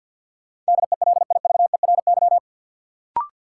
Morse code, heard as a keyed tone, about 53 wpm, 700 Hz. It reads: DELIVERY
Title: Sentences from Top 1500 Words - New Word Drill - 50wpm